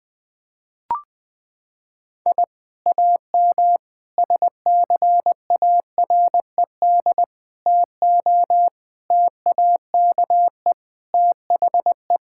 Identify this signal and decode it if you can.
Morse code, heard as a keyed tone, about 20 wpm, 700 Hz. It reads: I AM SCARED TO TAKE THE
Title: Sentences from Top 1000 Words - Review Part 5 - 20wpm